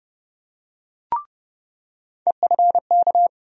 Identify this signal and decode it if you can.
Morse code, heard as a keyed tone, about 30 wpm, 700 Hz. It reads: EFK
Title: USA Call Sign Postfixes - 3 Letters 30wpm